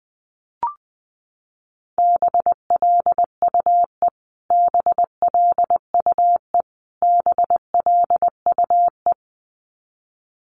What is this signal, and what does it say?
Morse code, heard as a keyed tone, about 20 wpm, 700 Hz. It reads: BLUE BLUE BLUE
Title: Top 500 Words Repeated 20wpm